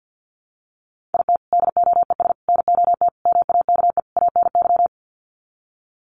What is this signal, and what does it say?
Morse code, heard as a keyed tone, about 50 wpm, 700 Hz. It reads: IT DOES NOT MAKE ANY
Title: Sets of 5 Words Encore2 50wpm